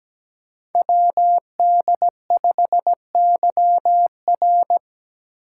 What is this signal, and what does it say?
Morse code, heard as a keyed tone, about 17 wpm, 700 Hz. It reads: WD5YR
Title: Code Groups - 5 Letter:Numbers - 17wpm